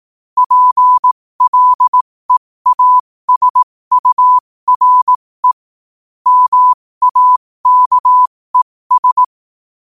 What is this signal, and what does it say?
Morse code, heard as a keyed tone, about 18 wpm, spaced as for 15 wpm, 1000 Hz. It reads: PLEASURE MAKES
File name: qotdfiles / 15wpm